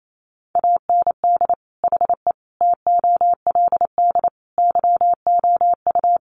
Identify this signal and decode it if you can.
Morse code, heard as a keyed tone, about 28 wpm, 700 Hz. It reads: AND HE TOLD YOU
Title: Sentences from Top 500 Words 28wpm